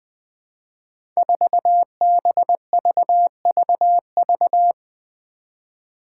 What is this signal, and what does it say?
Morse code, heard as a keyed tone, about 20 wpm, 700 Hz. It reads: 4BVVV
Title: Five Challenging Letter-Number Combinations 20wpm